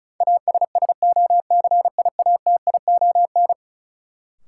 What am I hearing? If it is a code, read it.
Morse code, heard as a keyed tone, about 34 wpm, 700 Hz. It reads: ASSOCIATION
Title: Sentences from Top 1100 Words - New Word Drill Alt - 35wpm